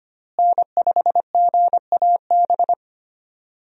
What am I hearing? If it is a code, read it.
Morse code, heard as a keyed tone, about 25 wpm, 700 Hz. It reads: N5GAB
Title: USA Call Signs - 5 Characters - Repeated 2x - 25wpm